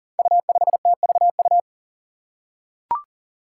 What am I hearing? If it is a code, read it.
Morse code, heard as a keyed tone, about 40 wpm, 700 Hz. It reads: U5TVU
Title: Five Challenging Letter-Number Combinations 40wpm